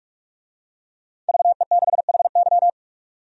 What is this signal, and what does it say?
Morse code, heard as a keyed tone, about 45 wpm, 700 Hz. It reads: VE6HY
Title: DXCC-100 Call Signs - 5 Characters - 45wpm